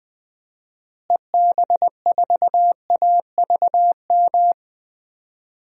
Morse code, heard as a keyed tone, about 20 wpm, 700 Hz. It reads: EB4AVM
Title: DXCC-100 Call Signs - 6 Characters - Repeated 2x - 20wpm